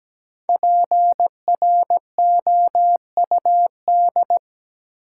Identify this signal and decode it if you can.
Morse code, heard as a keyed tone, about 17 wpm, 700 Hz. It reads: PROUD
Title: Five Letter Words - 17wpm